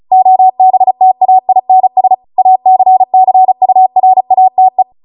Morse code, heard as a keyed tone, about 35 wpm, 750 Hz. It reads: OBTAINS ACCURATE
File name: Morse / 35WPM